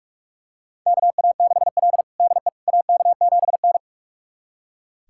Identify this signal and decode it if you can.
Morse code, heard as a keyed tone, about 42 wpm, 700 Hz. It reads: KA6L DE AK7N
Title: USA Callsign DE Callsign - 4 Characters - 45wpm